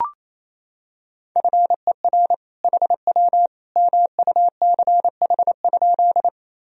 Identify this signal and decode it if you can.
Morse code, heard as a keyed tone, about 28 wpm, 700 Hz. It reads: FER HW MUCH?
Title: Sentences from Top 200 Words and Abbreviations 28wpm